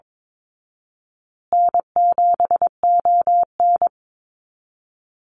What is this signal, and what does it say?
Morse code, heard as a keyed tone, about 22 wpm, 700 Hz. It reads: N7ON
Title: Call Signs Repeated 3x - 22wpm